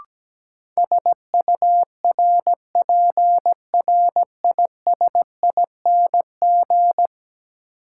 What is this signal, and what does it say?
Morse code, heard as a keyed tone, about 17 wpm, 700 Hz. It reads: SURPRISING
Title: Sets of 5 Words - New Word Drill - 17wpm